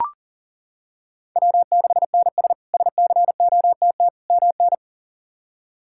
Morse code, heard as a keyed tone, about 40 wpm, 700 Hz. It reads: W6NS SCOTT MN
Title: K1USN SST QSO Practice - 40wpm